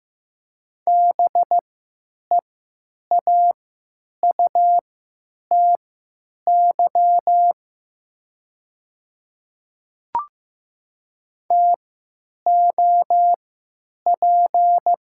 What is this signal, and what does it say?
Morse code, heard as a keyed tone, about 15 wpm, spaced as for 9 wpm, 700 Hz. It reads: BEAUTY TOP
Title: 3x Character Spacing - Top 500 Words - 15wpm